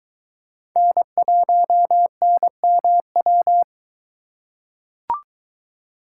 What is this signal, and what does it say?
Morse code, heard as a keyed tone, about 22 wpm, 700 Hz. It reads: N1NMW